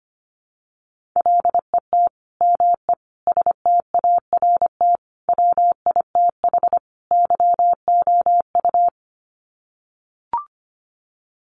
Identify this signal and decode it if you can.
Morse code, heard as a keyed tone, about 25 wpm, 700 Hz. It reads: LET ME START WITH YOU
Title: Sets of 5 Words Encore 25wpm